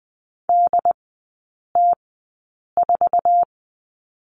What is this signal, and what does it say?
Morse code, heard as a keyed tone, about 20 wpm, spaced as for 8 wpm, 700 Hz. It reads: DT4
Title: 087 Lesson 15e - Number 5 - Three Characters 20wpm